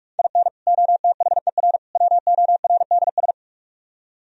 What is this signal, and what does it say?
Morse code, heard as a keyed tone, about 62 wpm, spaced as for 53 wpm, 700 Hz. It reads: IN OTHER WORDS
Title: Speed-Racing - Sets of 3 Words 45wpm